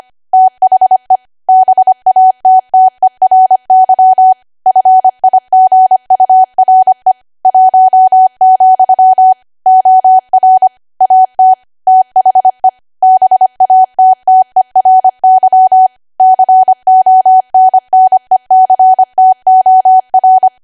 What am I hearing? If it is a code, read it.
Morse code, heard as a keyed tone, about 25 wpm, 750 Hz. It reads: THE BATTERY FIGURE 1, OR AT THE BATTERY CONNECTOR
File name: Archive / 25 WPM / 25WPM